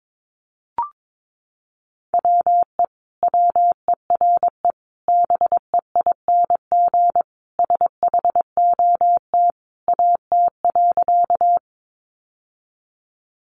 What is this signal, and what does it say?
Morse code, heard as a keyed tone, about 22 wpm, 700 Hz. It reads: WE WERE BEING SHOT AT.
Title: Sentences from Top 1100 Words 22wpm